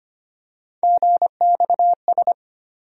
Morse code, heard as a keyed tone, about 25 wpm, 700 Hz. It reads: GXS
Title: USA Call Sign Postfixes - 3 Letters 25wpm